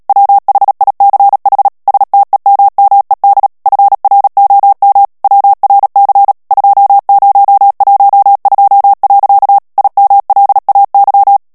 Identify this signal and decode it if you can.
Morse code, heard as a keyed tone, about 37 wpm, 800 Hz. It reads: WHICH STEMMED FROM WRC 2012. IMLAY
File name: morse mp3 / ANGLAIS texte clair 3 vitesse 35 mots minute